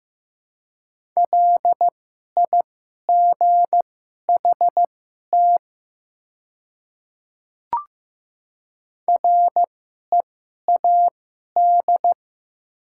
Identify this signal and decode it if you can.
Morse code, heard as a keyed tone, about 15 wpm, spaced as for 11 wpm, 700 Hz. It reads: LIGHT READ
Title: 2x Character Spacing - Top 300 Words - 15wpm